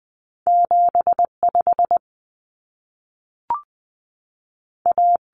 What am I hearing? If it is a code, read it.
Morse code, heard as a keyed tone, about 20 wpm, 700 Hz. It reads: 75 A
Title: Common Cut Numbers - 2-Digit 20wpm